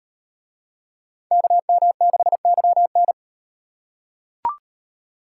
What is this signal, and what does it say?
Morse code, heard as a keyed tone, about 36 wpm, 700 Hz. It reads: KM6YN